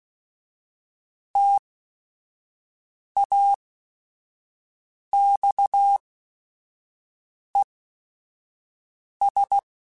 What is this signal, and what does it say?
Morse code, heard as a keyed tone, about 16 wpm, spaced as for 5 wpm, 800 Hz. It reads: TAXES
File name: T then H Runs / T then H Runs.mp3s / T then H Runs @5